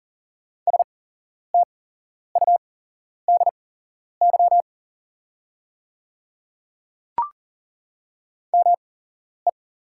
Morse code, heard as a keyed tone, about 40 wpm, spaced as for 11 wpm, 690 Hz. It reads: STUDY ME
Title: 8x Character Spacing - Top 300 Words - 40wpm